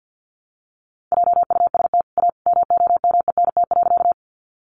Morse code, heard as a keyed tone, about 50 wpm, 700 Hz. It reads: JUST A MOMENT.
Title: Sentences from Top 1000 Words 50wpm